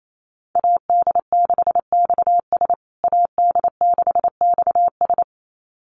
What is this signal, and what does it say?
Morse code, heard as a keyed tone, about 28 wpm, 700 Hz. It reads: AD6XS AD6XS
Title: USA Call Signs Repeated 2x 28wpm